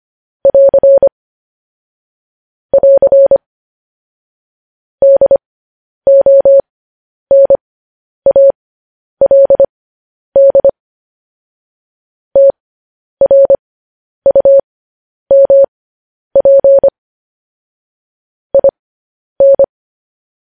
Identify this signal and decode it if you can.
Morse code, heard as a keyed tone, about 25 wpm, spaced as for 10 wpm, 550 Hz. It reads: <AR> <AR> DONALD TRUMP IN